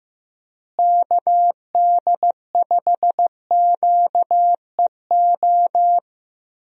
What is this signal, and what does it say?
Morse code, heard as a keyed tone, about 15 wpm, 700 Hz. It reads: KD5QEO
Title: USA Call Signs - 6 Characters - Repeated 3x - 15wpm